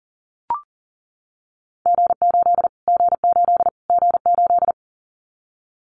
Morse code, heard as a keyed tone, about 40 wpm, 700 Hz. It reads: G8 G8 G8
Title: Callsign Prefixes for Top 25 DXCC - Repeated 3x - 40wpm